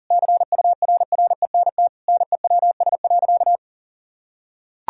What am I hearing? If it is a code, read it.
Morse code, heard as a keyed tone, about 40 wpm, 700 Hz. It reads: CURRENT NEWS.
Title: Sentences from Top 1000 Words - Review Part 4 - 40wpm